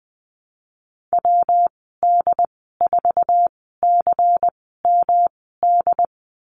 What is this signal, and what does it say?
Morse code, heard as a keyed tone, about 20 wpm, spaced as for 14 wpm, 700 Hz. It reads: WD4CMD
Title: 179 Lesson 29g - Slash - Call Signs 20wpm